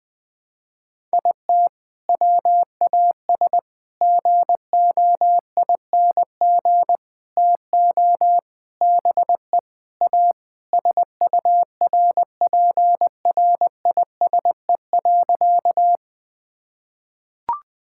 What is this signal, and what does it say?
Morse code, heard as a keyed tone, about 20 wpm, 700 Hz. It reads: IT WAS GOING TO BE A SURPRISE.
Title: Sentences with 4-Word Phrases - 20wpm